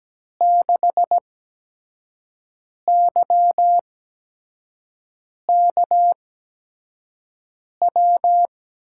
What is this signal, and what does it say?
Morse code, heard as a keyed tone, about 17 wpm, spaced as for 5 wpm, 700 Hz. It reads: 6YKW
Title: USA Call Signs - 6 Characters - 8x Character Spacing - 17wpm